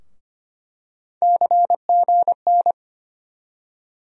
Morse code, heard as a keyed tone, about 25 wpm, 700 Hz. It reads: CGN
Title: USA Call Sign Postfixes - 3 Letters 25wpm